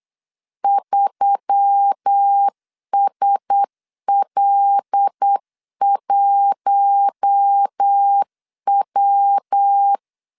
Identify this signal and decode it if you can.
Morse code, heard as a keyed tone, about 8 wpm, 800 Hz. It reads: 3SL1W
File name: CW / DET / 1 6 1500C S C 8wpm 5219